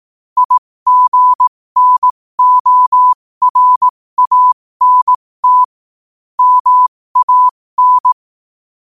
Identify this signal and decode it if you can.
Morse code, heard as a keyed tone, about 18 wpm, spaced as for 15 wpm, 1000 Hz. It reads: IGNORANT MAN
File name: qotdfiles / 15wpm